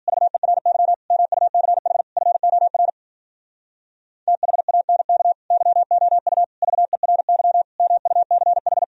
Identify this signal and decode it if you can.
Morse code, heard as a keyed tone, about 51 wpm, 700 Hz. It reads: VERY MUCH FOR THANK YOU VERY MUCH